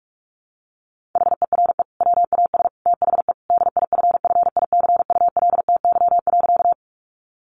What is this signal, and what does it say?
Morse code, heard as a keyed tone, about 44 wpm, 700 Hz. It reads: HERE WAS THE DIFFICULTY.
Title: Sentences from Top 1500 Words 45wpm